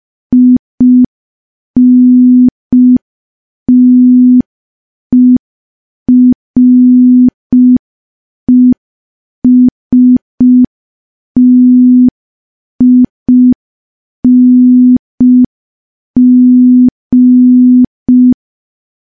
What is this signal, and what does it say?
Morse code, heard as a keyed tone, about 5 wpm, 250 Hz. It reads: INTERESTING
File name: morse / morse-training-05WPM